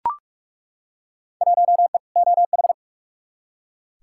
Morse code, heard as a keyed tone, about 42 wpm, 700 Hz. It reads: 1E OH
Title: Field Day Exchanges - 45wpm